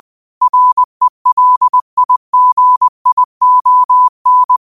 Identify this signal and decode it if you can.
Morse code, heard as a keyed tone, about 20 wpm, 1000 Hz. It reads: RELIGION